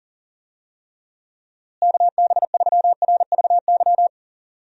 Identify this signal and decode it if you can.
Morse code, heard as a keyed tone, about 40 wpm, 700 Hz. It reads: KB3RVY
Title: USA Call Signs - 6 Characters - Repeated 3x - 40wpm